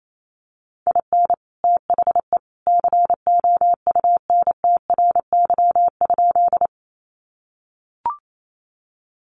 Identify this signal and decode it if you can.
Morse code, heard as a keyed tone, about 28 wpm, 700 Hz. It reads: IN THE COUNTRY?
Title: Sentences from Top 300 Words 28wpm